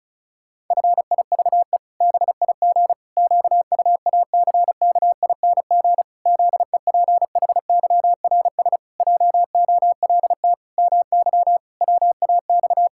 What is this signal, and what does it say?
Morse code, heard as a keyed tone, about 36 wpm, 700 Hz. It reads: FIVE BIG QUACKING ZEPHYRS JOLT MY WAX
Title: Pangram Challenge v2 - 35wpm